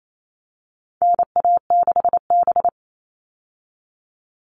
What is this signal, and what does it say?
Morse code, heard as a keyed tone, about 28 wpm, 700 Hz. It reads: NA6B
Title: All Cut Numbers - 4-Digit 28wpm